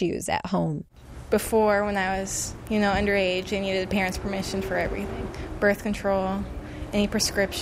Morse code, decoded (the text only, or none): none